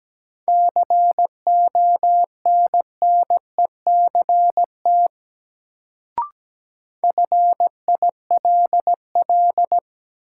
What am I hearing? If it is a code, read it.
Morse code, heard as a keyed tone, about 17 wpm, 700 Hz. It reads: CONNECT FILL
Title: Sentences from Top 1500 Words - New Word Drill Alt - 17wpm